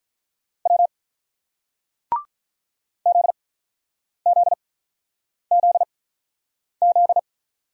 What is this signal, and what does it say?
Morse code, heard as a keyed tone, about 49 wpm, spaced as for 9 wpm, 690 Hz. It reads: W ZZZZ